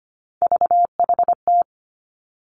Morse code, heard as a keyed tone, about 25 wpm, 700 Hz. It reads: VHT
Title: Three Challenging Letter-Number Combinations 25wpm